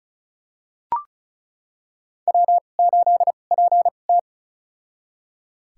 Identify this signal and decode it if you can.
Morse code, heard as a keyed tone, about 36 wpm, spaced as for 25 wpm, 700 Hz. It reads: W8PT
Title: USA Call Signs - 2x Character Spacing - 35wpm